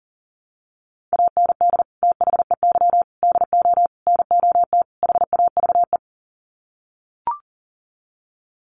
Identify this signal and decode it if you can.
Morse code, heard as a keyed tone, about 40 wpm, 700 Hz. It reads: AND THEY DO NOT HAVE
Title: Sets of 5 Words Encore2 40wpm